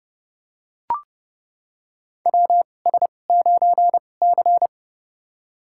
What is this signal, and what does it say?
Morse code, heard as a keyed tone, about 32 wpm, spaced as for 22 wpm, 700 Hz. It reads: WS9C
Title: USA Call Signs - 4 Characters - 2x Character Spacing - 30wpm